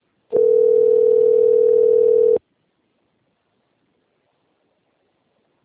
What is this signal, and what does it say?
Ringback tone (North America), 440 + 480 Hz.
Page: Londrina - PR